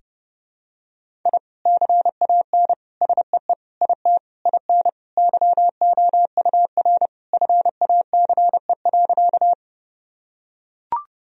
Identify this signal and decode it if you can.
Morse code, heard as a keyed tone, about 30 wpm, 700 Hz. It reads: I CAN SEE IT IN YOUR FACE.
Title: Sentences from Top 1000 Words - Review Part 2 - 30wpm